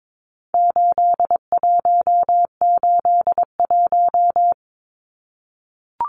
Morse code, heard as a keyed tone, about 22 wpm, 700 Hz. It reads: 8181
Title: Numbers - 4-Digit 22wpm